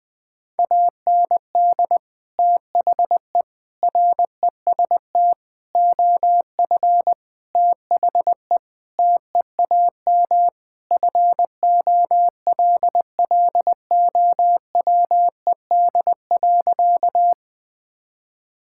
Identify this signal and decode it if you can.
Morse code, heard as a keyed tone, about 20 wpm, 700 Hz. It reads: AND THE REST OF THE TEAM FOLLOWED.
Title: Sentences with 5-Word Phrases - Spoken Beginning - 20wpm